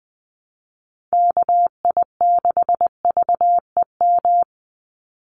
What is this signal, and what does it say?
Morse code, heard as a keyed tone, about 20 wpm, 700 Hz. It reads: KI6VEM
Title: USA Call Signs 20wpm